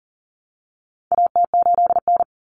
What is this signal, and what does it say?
Morse code, heard as a keyed tone, about 40 wpm, 700 Hz. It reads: AT8N
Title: Common Cut Numbers - 4-Digit 40wpm